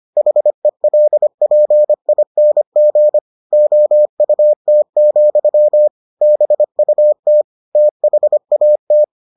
Morse code, heard as a keyed tone, about 25 wpm, 600 Hz. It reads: HELPING OUT, BUT THAT